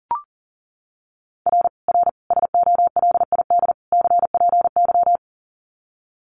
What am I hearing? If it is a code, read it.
Morse code, heard as a keyed tone, about 40 wpm, 700 Hz. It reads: R R SOLID CPY
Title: Common QSO Phrases 40wpm